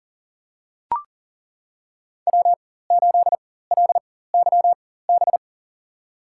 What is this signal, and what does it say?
Morse code, heard as a keyed tone, about 40 wpm, spaced as for 19 wpm, 700 Hz. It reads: W8LYB
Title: USA Call Signs - 5 Characters - 4x Character Spacing - 40wpm